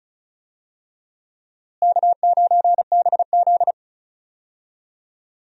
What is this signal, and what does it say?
Morse code, heard as a keyed tone, about 34 wpm, 700 Hz. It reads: K9BZ
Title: USA Call Signs - 4 Characters - Repeated 3x - 35wpm